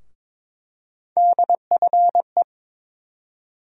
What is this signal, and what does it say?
Morse code, heard as a keyed tone, about 22 wpm, 700 Hz. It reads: DFE
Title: USA Call Sign Postfixes - 3 Letters 22wpm